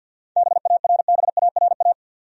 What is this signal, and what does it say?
Morse code, heard as a keyed tone, about 46 wpm, 700 Hz. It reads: BARBARA
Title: USA Common Names - 50wpm